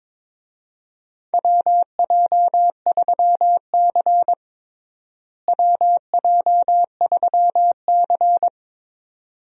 Morse code, heard as a keyed tone, about 22 wpm, 700 Hz. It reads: WJ3C WJ3C
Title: USA Call Signs - 4 Characters - Repeated 2x - 22wpm